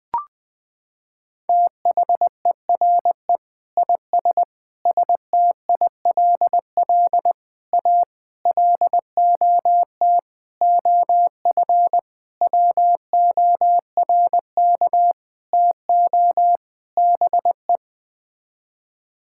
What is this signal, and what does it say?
Morse code, heard as a keyed tone, about 20 wpm, 700 Hz. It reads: THERE IS STILL A LOT OF WORK TO BE
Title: Sets of 9 Words - 20wpm